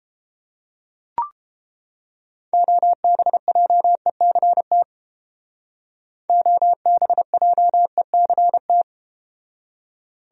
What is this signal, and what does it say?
Morse code, heard as a keyed tone, about 32 wpm, 700 Hz. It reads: OBJECT OBJECT